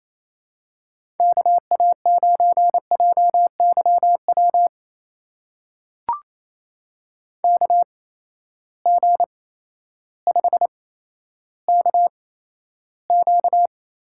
Morse code, heard as a keyed tone, about 28 wpm, spaced as for 13 wpm, 700 Hz. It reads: KA9JYW KG5KQ
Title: USA Call Signs - 6 Characters - 8x Character Spacing - 28wpm